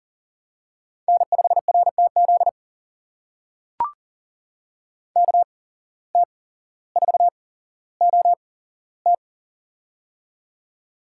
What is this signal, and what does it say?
Morse code, heard as a keyed tone, about 42 wpm, spaced as for 17 wpm, 700 Hz. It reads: N5RTZ KT4OT
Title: USA Call Signs - 5 Characters - 8x Character Spacing - 40wpm